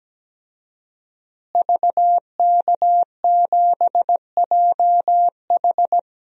Morse code, heard as a keyed tone, about 17 wpm, 700 Hz. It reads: VK7JH
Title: DXCC-100 Call Signs - 5 Characters - Repeated 2x - 17wpm